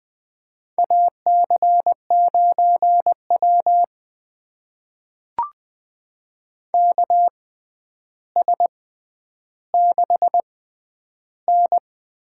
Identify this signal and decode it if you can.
Morse code, heard as a keyed tone, about 20 wpm, spaced as for 10 wpm, 700 Hz. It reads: AC9W KS6N
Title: USA Call Signs - 4 Characters - 6x Character Spacing - 20wpm